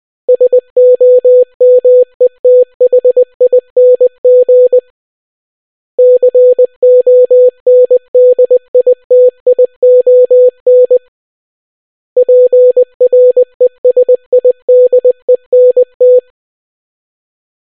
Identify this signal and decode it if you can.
Morse code, heard as a keyed tone, about 20 wpm, 500 Hz. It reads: SOMETHING CONDITION PRESIDENT